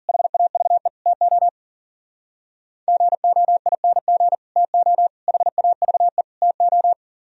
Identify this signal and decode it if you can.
Morse code, heard as a keyed tone, about 51 wpm, 700 Hz. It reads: HAVE TO GOING TO HAVE TO